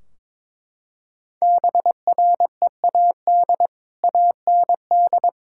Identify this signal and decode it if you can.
Morse code, heard as a keyed tone, about 21 wpm, 700 Hz. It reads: BREAD AND
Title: Sentences from Top 1600 Words 22wpm